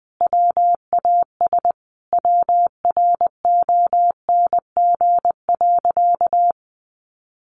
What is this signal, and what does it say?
Morse code, heard as a keyed tone, about 20 wpm, 700 Hz. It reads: WAS WRONG.
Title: Sentences from Top 900 Words 20wpm